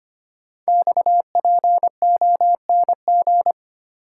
Morse code, heard as a keyed tone, about 25 wpm, 700 Hz. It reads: XPONG
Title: Code Groups - 5 Letters - 25wpm